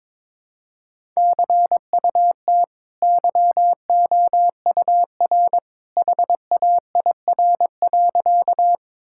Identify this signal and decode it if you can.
Morse code, heard as a keyed tone, about 22 wpm, 690 Hz. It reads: CUT YOUR HAIR.
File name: practice / 22wpm / Easy Sentences with Hints - 22wpm